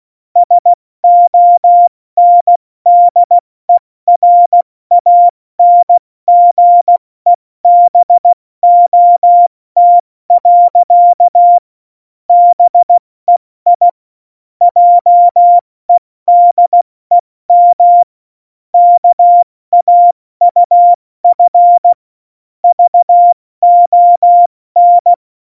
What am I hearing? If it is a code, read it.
Morse code, heard as a keyed tone, about 16 wpm, 700 Hz. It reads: SONDERANGEBOT. BEI JEDEM KAUF VON
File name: Morsekurs / Witze / Witz 15. Januar 2019, 70 F80